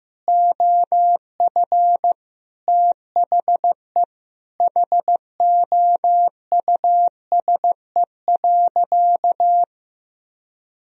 Morse code, heard as a keyed tone, about 14 wpm, 700 Hz. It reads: OF THE HOUSE.
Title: Sentences from Top 1000 Words - Review Part 1 - 15wpm